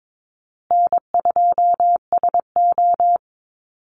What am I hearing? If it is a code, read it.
Morse code, heard as a keyed tone, about 22 wpm, 700 Hz. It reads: N2SO
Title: Easy Call Signs Repeated 2x - 22wpm